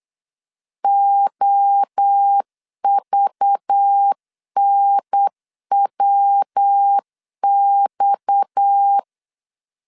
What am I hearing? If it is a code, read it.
Morse code, heard as a keyed tone, about 8 wpm, 800 Hz. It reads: OVNWX
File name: LEZIONI / 6 PKX0 / 6 PKX0 8wpm 3341